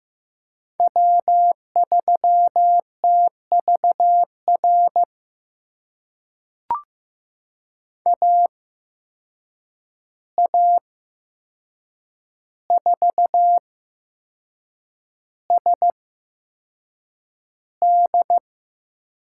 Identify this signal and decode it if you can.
Morse code, heard as a keyed tone, about 15 wpm, spaced as for 6 wpm, 700 Hz. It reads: W3TVR AA4SD
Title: USA Call Signs - 5 Characters - 8x Character Spacing - 15wpm